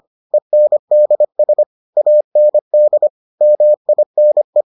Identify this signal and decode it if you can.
Morse code, heard as a keyed tone, about 25 wpm, 600 Hz. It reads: ENDS AND MINE